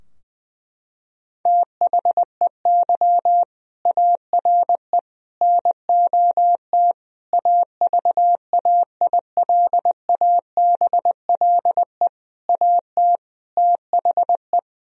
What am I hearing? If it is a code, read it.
Morse code, heard as a keyed tone, about 20 wpm, 700 Hz. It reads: THEY ARE NOT AVAILABLE AT THE
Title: Sentences from Top 1000 Words - Review Part 3 - 20wpm